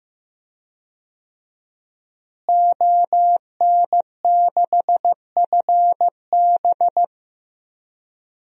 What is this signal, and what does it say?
Morse code, heard as a keyed tone, about 15 wpm, 700 Hz. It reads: ON6FB
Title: DXCC-100 Call Signs - 5 Characters - Repeated 3x - 15wpm